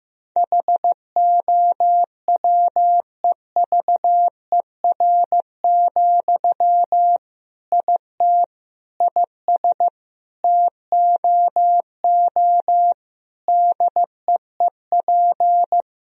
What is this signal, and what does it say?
Morse code, heard as a keyed tone, about 15 wpm, 700 Hz. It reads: HOWEVER, IT IS TOO DEEP
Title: Sentences from Top 1000 Words - Review Part 3 - 15wpm